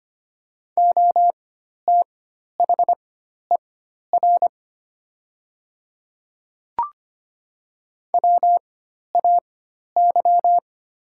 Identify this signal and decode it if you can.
Morse code, heard as a keyed tone, about 25 wpm, spaced as for 12 wpm, 700 Hz. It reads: OTHER WAY
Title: 4x Character Spacing - Top 100 Words - 25wpm